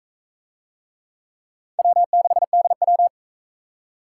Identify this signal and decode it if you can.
Morse code, heard as a keyed tone, about 41 wpm, 700 Hz. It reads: W6DW